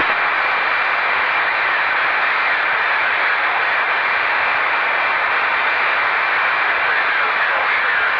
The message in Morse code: none